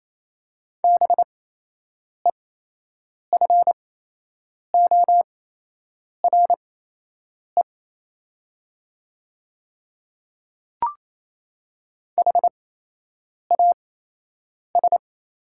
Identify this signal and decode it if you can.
Morse code, heard as a keyed tone, about 27 wpm, spaced as for 8 wpm, 700 Hz. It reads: BEFORE HAS
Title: 8x Character Spacing - Top 300 Words - 28wpm